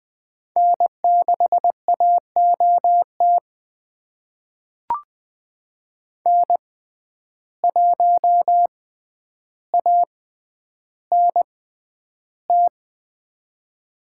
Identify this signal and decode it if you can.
Morse code, heard as a keyed tone, about 21 wpm, spaced as for 10 wpm, 700 Hz. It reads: N6AOT N1ANT
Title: USA Call Signs - 5 Characters - 6x Character Spacing - 20wpm